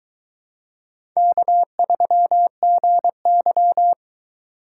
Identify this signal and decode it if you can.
Morse code, heard as a keyed tone, about 23 wpm, 700 Hz. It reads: K3GY